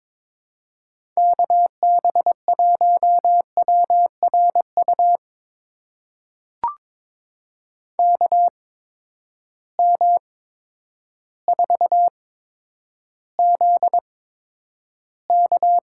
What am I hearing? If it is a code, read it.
Morse code, heard as a keyed tone, about 22 wpm, spaced as for 10 wpm, 700 Hz. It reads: KB1WRU KM4ZK
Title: USA Call Signs - 6 Characters - 8x Character Spacing - 22wpm